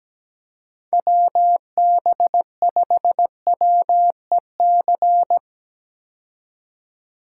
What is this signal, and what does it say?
Morse code, heard as a keyed tone, about 17 wpm, 700 Hz. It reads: WB5WEC
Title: K1USN SST QSO Practice - 17wpm